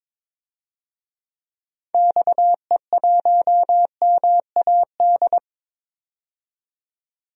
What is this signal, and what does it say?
Morse code, heard as a keyed tone, about 22 wpm, 700 Hz. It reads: XE1MAD
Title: DXCC-100 Call Signs - 6 Characters - 22wpm